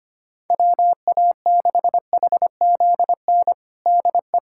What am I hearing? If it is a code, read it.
Morse code, heard as a keyed tone, about 25 wpm, 700 Hz. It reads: WA6HZN DE
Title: USA Callsign DE Callsign - 6 Characters - 25wpm